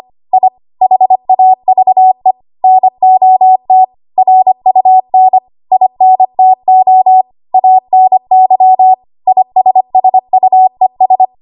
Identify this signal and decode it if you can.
Morse code, heard as a keyed tone, about 25 wpm, 750 Hz. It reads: I HAVE NOT RUN INTO ANY ISSUES
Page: 25 WPM Code Archive